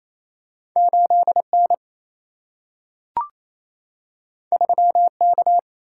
Morse code, heard as a keyed tone, about 28 wpm, 700 Hz. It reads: 8N 3K
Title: Single-Letter Single-Number 28wpm